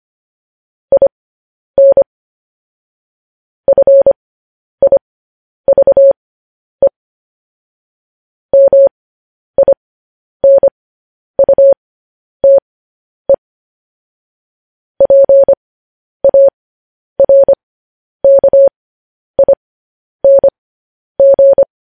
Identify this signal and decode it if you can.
Morse code, heard as a keyed tone, about 25 wpm, spaced as for 10 wpm, 550 Hz. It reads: IN FIVE MINUTE PARKING
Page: news headlines in morse code at 10 wpm